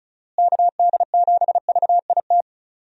Morse code, heard as a keyed tone, about 36 wpm, 700 Hz. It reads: KD7VIT
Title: USA Call Signs - 6 Characters - Repeated 2x - 35wpm